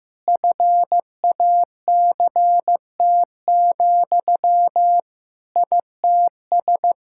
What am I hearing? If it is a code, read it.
Morse code, heard as a keyed tone, about 16 wpm, 700 Hz. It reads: FACT, ITS
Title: Sentences from Top 1000 Words - Review Part 5 - 15wpm